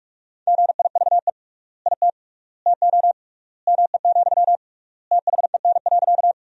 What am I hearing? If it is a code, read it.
Morse code, heard as a keyed tone, about 46 wpm, 700 Hz. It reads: GIVE IT TO ME, THEN.
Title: 3x Word-Spacing - Sentences from Top 200 Words - 45wpm